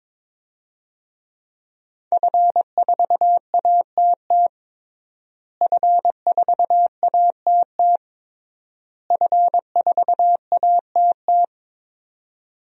Morse code, heard as a keyed tone, about 22 wpm, 700 Hz. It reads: F4ATT F4ATT F4ATT
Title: DXCC-100 Call Signs - 5 Characters - Repeated 3x - 22wpm